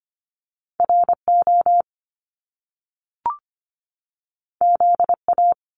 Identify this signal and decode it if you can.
Morse code, heard as a keyed tone, about 25 wpm, 700 Hz. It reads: RO ZA
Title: Top Two Letter Combinations in Words 25wpm